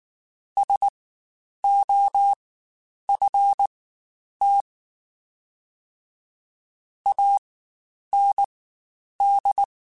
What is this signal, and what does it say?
Morse code, heard as a keyed tone, about 19 wpm, spaced as for 9 wpm, 800 Hz. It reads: SOFT AND
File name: Spaced / Chuckles Spaced-mp3s / SoftWords sp @18